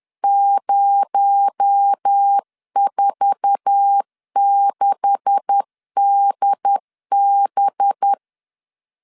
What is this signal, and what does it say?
Morse code, heard as a keyed tone, about 10 wpm, 800 Hz. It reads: 046DB
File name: LEZIONI / 9 286 10wpm 2735